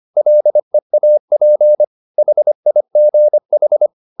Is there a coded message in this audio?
Yes: Morse code from 0.2 to 3.9 s, heard as a keyed tone, about 25 wpm, 600 Hz. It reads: LEAP HIGH